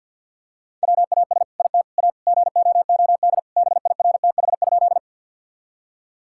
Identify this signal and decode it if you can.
Morse code, heard as a keyed tone, about 49 wpm, 700 Hz. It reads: WAS IT A GOOD BIRTH?
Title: Sentences from Top 1000 Words - Review Part 5 - 50wpm